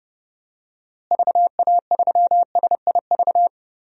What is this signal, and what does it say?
Morse code, heard as a keyed tone, about 30 wpm, 700 Hz. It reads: VA3SIV
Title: DXCC-100 Call Signs - 6 Characters - Repeated 2x - 30wpm